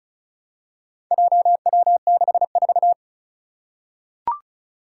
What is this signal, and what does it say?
Morse code, heard as a keyed tone, about 36 wpm, 700 Hz. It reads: JW64
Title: Four Challenging Letter-Number Combinations 35wpm